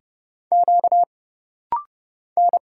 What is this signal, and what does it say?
Morse code, heard as a keyed tone, about 31 wpm, 700 Hz. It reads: Q N
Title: Single Letter-Number - Rapid-Fire 30wpm